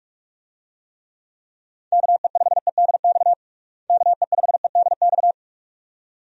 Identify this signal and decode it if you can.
Morse code, heard as a keyed tone, about 46 wpm, 700 Hz. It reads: KE5EDX KE5EDX
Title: USA Call Signs - 6 Characters - Repeated 2x - 45wpm